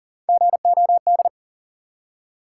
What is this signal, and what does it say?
Morse code, heard as a keyed tone, about 40 wpm, 700 Hz. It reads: GOD
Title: Three Letter Words - 40wpm